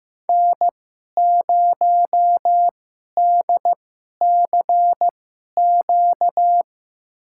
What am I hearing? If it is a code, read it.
Morse code, heard as a keyed tone, about 15 wpm, spaced as for 11 wpm, 700 Hz. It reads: N0DCQ
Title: USA Call Signs - 5 Characters - 2x Character Spacing - 15wpm